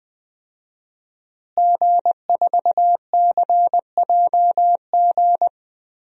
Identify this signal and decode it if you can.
Morse code, heard as a keyed tone, about 20 wpm, 700 Hz. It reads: G4CJG
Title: DXCC-100 Call Signs - 5 Characters - Repeated 2x - 20wpm